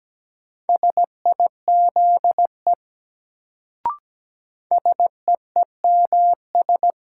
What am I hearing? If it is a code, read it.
Morse code, heard as a keyed tone, about 17 wpm, 700 Hz. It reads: SIZE SEEMS
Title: Sets of 3 Words - New Word Drill - 17wpm